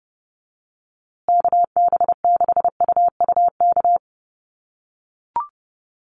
Morse code, heard as a keyed tone, about 28 wpm, 700 Hz. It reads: KB6UUK
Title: USA Call Signs Repeated 3x 30wpm